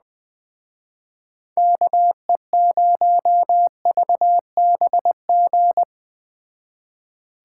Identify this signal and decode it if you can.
Morse code, heard as a keyed tone, about 20 wpm, 700 Hz. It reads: KE0VBG